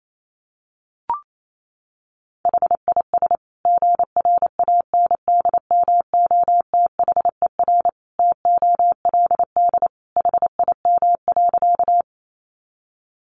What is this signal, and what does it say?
Morse code, heard as a keyed tone, about 28 wpm, 700 Hz. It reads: HIS GRANDMOTHER TOLD HIM.
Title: Sentences from Top 1200 Words 28wpm